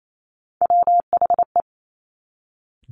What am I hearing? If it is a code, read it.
Morse code, heard as a keyed tone, about 28 wpm, 700 Hz. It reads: WHE
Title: Top Three Letter Combinations in Words 28wpm